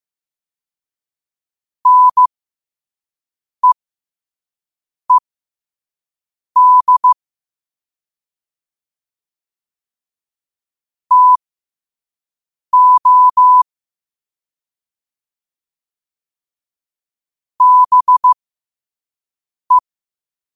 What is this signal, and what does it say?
Morse code, heard as a keyed tone, about 15 wpm, spaced as for 5 wpm, 1000 Hz. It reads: NEED TO BE
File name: qotdfiles / 5wpm